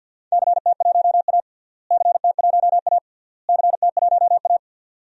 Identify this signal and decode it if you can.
Morse code, heard as a keyed tone, about 50 wpm, 700 Hz. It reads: CT1A CT1A CT1A
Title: DXCC-100 Call Signs - 4 Characters - Repeated 3x - 50wpm